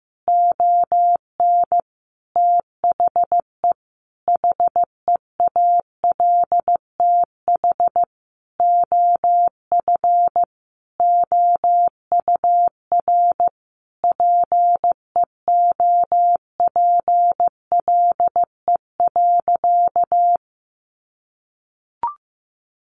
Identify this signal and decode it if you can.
Morse code, heard as a keyed tone, about 15 wpm, 700 Hz. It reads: ON THE HEALTH OF OUR PEOPLE.
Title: Sentences from Top 1500 Words 15wpm